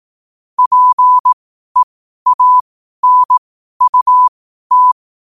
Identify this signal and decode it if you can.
Morse code, heard as a keyed tone, about 18 wpm, spaced as for 12 wpm, 1000 Hz. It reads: PEANUT
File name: qotdfiles / 12wpm